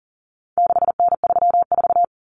Morse code, heard as a keyed tone, about 40 wpm, 700 Hz. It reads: 6N34
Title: Common Cut Numbers - 4-Digit 40wpm